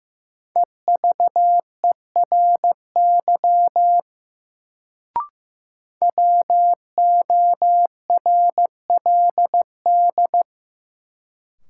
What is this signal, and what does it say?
Morse code, heard as a keyed tone, about 15 wpm, 700 Hz. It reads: EVERY WORLD
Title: Sentences from Top 200 Words - New Word Drill - 15wpm